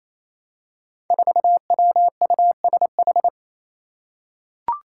Morse code, heard as a keyed tone, about 27 wpm, 700 Hz. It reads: 4WUSH
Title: Five Challenging Letter-Number Combinations 28wpm